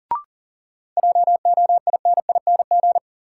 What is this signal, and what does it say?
Morse code, heard as a keyed tone, about 40 wpm, 700 Hz. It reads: JOINING
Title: Sets of 5 Words - New Word Drill - 40wpm